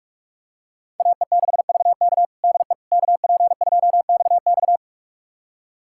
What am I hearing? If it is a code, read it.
Morse code, heard as a keyed tone, about 46 wpm, 700 Hz. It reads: AE6VK DE KP2XX
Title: USA Callsign DE Callsign - 5 Characters - 45wpm